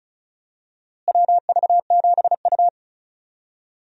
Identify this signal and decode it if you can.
Morse code, heard as a keyed tone, about 34 wpm, 700 Hz. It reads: WV7U
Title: USA Call Signs - 4 Characters - Repeated 3x - 35wpm